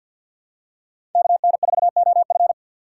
Morse code, heard as a keyed tone, about 51 wpm, 700 Hz. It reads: KN4OF
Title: USA Call Signs - 3x Character Spacing - 50wpm